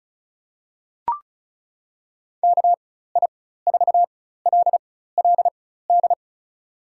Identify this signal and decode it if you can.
Morse code, heard as a keyed tone, about 36 wpm, spaced as for 16 wpm, 700 Hz. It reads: KI4LLD
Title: USA Call Signs - 6 Characters - 4x Character Spacing - 35wpm